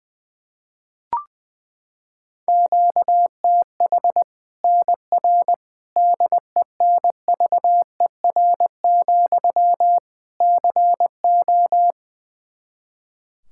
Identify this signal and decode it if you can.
Morse code, heard as a keyed tone, about 20 wpm, 700 Hz. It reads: QTH NR DENVER, CO
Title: K1USN SST QSO Practice 20wpm